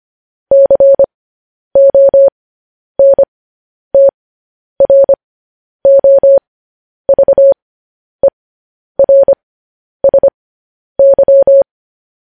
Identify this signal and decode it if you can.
Morse code, heard as a keyed tone, about 25 wpm, spaced as for 10 wpm, 550 Hz. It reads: CONTROVERSY